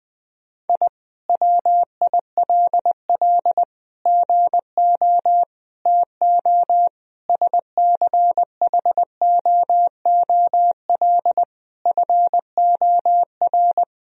Morse code, heard as a keyed tone, about 20 wpm, 700 Hz. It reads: I WILL GO TO SCHOOL FOR
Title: Sentences from Top 1000 Words - Review Part 4 - 20wpm